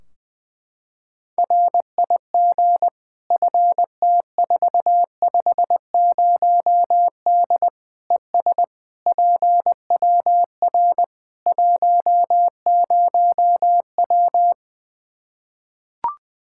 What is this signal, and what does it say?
Morse code, heard as a keyed tone, about 20 wpm, 700 Hz. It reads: RIG FT450D ES PWR 10W
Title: Practice QSO - Exchange 2 - Info - 20wpm